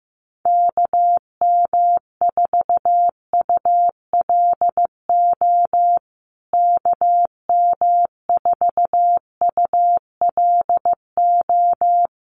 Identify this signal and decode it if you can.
Morse code, heard as a keyed tone, about 15 wpm, 700 Hz. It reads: KM4ULO KM4ULO
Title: USA Call Signs Repeated 3x 15wpm